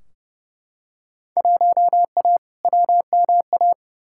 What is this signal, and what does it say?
Morse code, heard as a keyed tone, about 32 wpm, 700 Hz. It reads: 1A WMA
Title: Field Day Exchanges - 30wpm